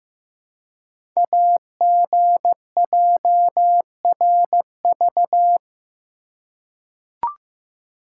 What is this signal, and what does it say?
Morse code, heard as a keyed tone, about 15 wpm, 700 Hz. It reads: AGJRV
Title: Code Groups - 5 Letter:Numbers - 15wpm